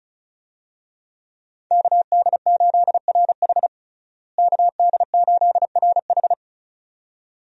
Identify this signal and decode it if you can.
Morse code, heard as a keyed tone, about 36 wpm, 700 Hz. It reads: KD8RH KD8RH
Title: USA Call Signs - 5 Characters - Repeated 2x - 35wpm